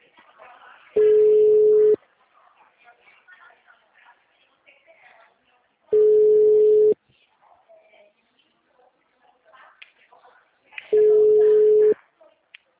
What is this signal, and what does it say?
Ringback tone (Europe, 425 Hz), with speech.